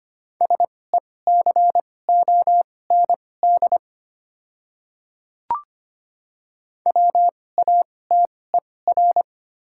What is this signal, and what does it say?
Morse code, heard as a keyed tone, about 25 wpm, spaced as for 18 wpm, 700 Hz. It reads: SECOND WATER
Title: 2x Character Spacing - Top 300 Words - 25wpm